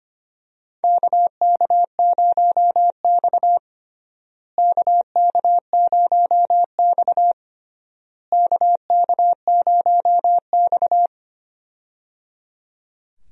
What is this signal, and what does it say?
Morse code, heard as a keyed tone, about 25 wpm, 700 Hz. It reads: KK0X KK0X KK0X
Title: USA Call Signs - 4 Characters - Repeated 3x - 25wpm